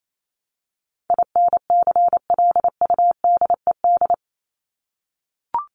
Morse code, heard as a keyed tone, about 28 wpm, 700 Hz. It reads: INCLUDED
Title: ED Suffix - Top 100 Words with Base Words 28wpm